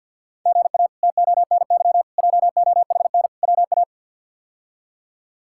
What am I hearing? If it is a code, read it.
Morse code, heard as a keyed tone, about 46 wpm, 700 Hz. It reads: GA TONY JOHN WA
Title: K1USN SST QSO Practice - 50wpm